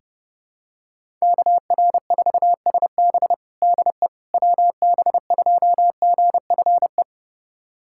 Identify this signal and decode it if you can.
Morse code, heard as a keyed tone, about 30 wpm, 700 Hz. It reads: KR4SB DE WB2GFE
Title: USA Callsign DE Callsign 30wpm